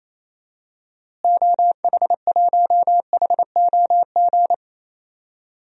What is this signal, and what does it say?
Morse code, heard as a keyed tone, about 28 wpm, 700 Hz. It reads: OH1HOG
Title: DXCC-100 Call Signs - 6 Characters - Repeated 2x - 28wpm